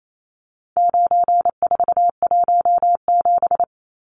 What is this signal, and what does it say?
Morse code, heard as a keyed tone, about 28 wpm, 700 Hz. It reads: 9417
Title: Numbers - 4-Digit 28wpm